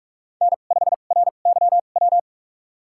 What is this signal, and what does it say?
Morse code, heard as a keyed tone, about 46 wpm, spaced as for 33 wpm, 700 Hz. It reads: N5RYW
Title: USA Call Signs - 5 Characters - 2x Character Spacing - 45wpm